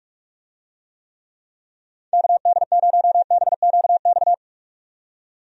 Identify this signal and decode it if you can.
Morse code, heard as a keyed tone, about 46 wpm, 700 Hz. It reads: KD0BQX
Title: USA Call Signs - 6 Characters - 45wpm